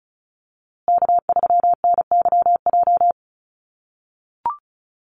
Morse code, heard as a keyed tone, about 34 wpm, 700 Hz. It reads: K3NYJ
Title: USA Call Signs Repeated 3x 35wpm